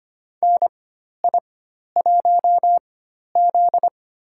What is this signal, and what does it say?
Morse code, heard as a keyed tone, about 25 wpm, spaced as for 12 wpm, 690 Hz. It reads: NI1Z
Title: USA Call Signs - 4 Characters - 4x Character Spacing - 25wpm